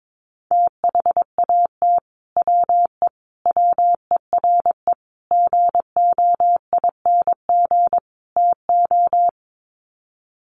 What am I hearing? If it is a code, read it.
Morse code, heard as a keyed tone, about 22 wpm, 700 Hz. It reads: THAT WE WERE GOING TO
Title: Sets of 5 Words Encore2 22wpm